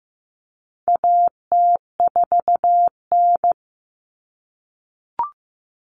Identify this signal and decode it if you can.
Morse code, heard as a keyed tone, about 15 wpm, 700 Hz. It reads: AT4N
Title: Common Cut Numbers - 4-Digit 15wpm